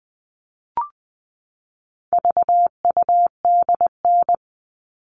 Morse code, heard as a keyed tone, about 20 wpm, 700 Hz. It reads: VUDN
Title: All Cut Numbers - 4-Digit 20wpm